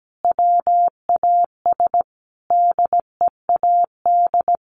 Morse code, heard as a keyed tone, about 17 wpm, 700 Hz. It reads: WAS DEAD
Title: Sentences from Top 1500 Words 17wpm